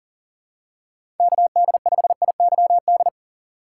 Morse code, heard as a keyed tone, about 42 wpm, 700 Hz. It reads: KD5IYD
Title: USA Call Signs - 6 Characters - 6x Character Spacing - 40wpm